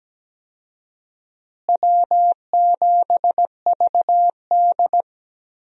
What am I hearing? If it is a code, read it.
Morse code, heard as a keyed tone, about 17 wpm, 700 Hz. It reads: W7VD
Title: USA Call Signs - 4 Characters - Repeated 2x - 17wpm